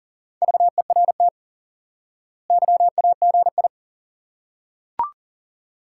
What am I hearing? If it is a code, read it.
Morse code, heard as a keyed tone, about 40 wpm, 700 Hz. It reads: VERT YAGI
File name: IQE Course 3 - Lesson 89 - YAGI - Two Elements - 40wpm